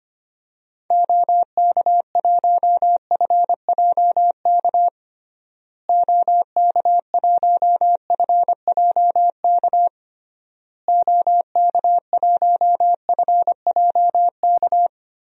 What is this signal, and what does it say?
Morse code, heard as a keyed tone, about 25 wpm, 700 Hz. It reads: OK1FJK OK1FJK OK1FJK
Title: DXCC-100 Call Signs - 6 Characters - Repeated 3x - 25wpm